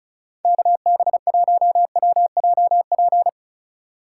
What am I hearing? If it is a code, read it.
Morse code, heard as a keyed tone, about 36 wpm, 700 Hz. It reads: KB1WJP
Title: USA Call Signs - 6 Characters - 4x Character Spacing - 35wpm